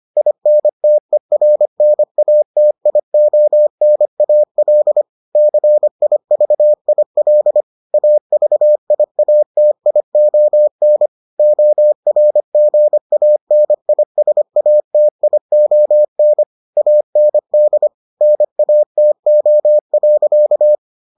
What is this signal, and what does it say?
Morse code, heard as a keyed tone, about 25 wpm, 600 Hz. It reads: INTERNATIONAL CIVIL AVIATION ORGANISATION AND NATO.